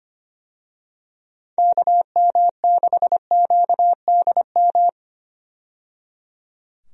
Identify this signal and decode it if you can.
Morse code, heard as a keyed tone, about 25 wpm, 700 Hz. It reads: KM6QDM
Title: USA Call Signs - 6 Characters - 25wpm